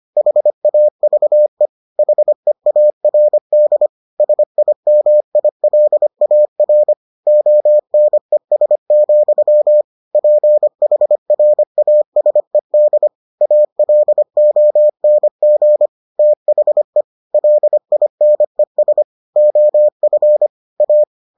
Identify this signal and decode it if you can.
Morse code, heard as a keyed tone, about 25 wpm, 600 Hz. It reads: HAVE HEARD SIMILAR ONES, PHRASED ALONG THE LINES OF A